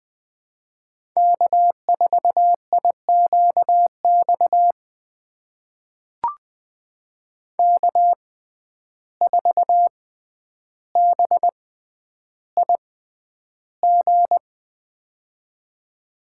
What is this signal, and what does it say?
Morse code, heard as a keyed tone, about 21 wpm, spaced as for 10 wpm, 700 Hz. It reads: K4IQX K4BIG
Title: USA Call Signs - 5 Characters - 6x Character Spacing - 20wpm